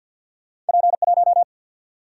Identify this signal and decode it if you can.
Morse code, heard as a keyed tone, about 53 wpm, 700 Hz. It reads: P 1
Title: Single-Letter Single-Number 50wpm